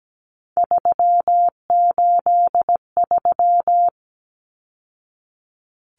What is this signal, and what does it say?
Morse code, heard as a keyed tone, about 17 wpm, 700 Hz. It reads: 383
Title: Numbers - 3-Digit 17wpm